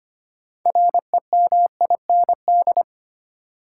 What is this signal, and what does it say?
Morse code, heard as a keyed tone, about 25 wpm, 700 Hz. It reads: REMIND
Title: Six Letter Words - 25wpm